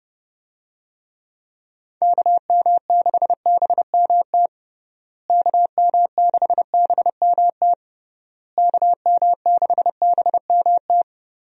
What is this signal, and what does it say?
Morse code, heard as a keyed tone, about 32 wpm, 700 Hz. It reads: KM6BMT KM6BMT KM6BMT
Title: USA Call Signs - 6 Characters - Repeated 3x - 30wpm